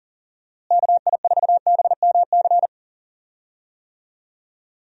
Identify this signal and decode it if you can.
Morse code, heard as a keyed tone, about 40 wpm, 700 Hz. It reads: KI4BMC
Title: USA Call Signs - 6 Characters - Repeated 3x - 40wpm